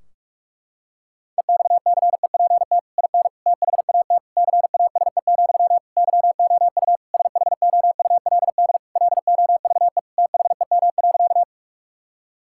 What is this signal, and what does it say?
Morse code, heard as a keyed tone, about 46 wpm, 700 Hz. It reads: EXCEPT IN THAT CASE, YOU SHOULD LOVE THEM.
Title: Sentences from Top 1600 Words 45wpm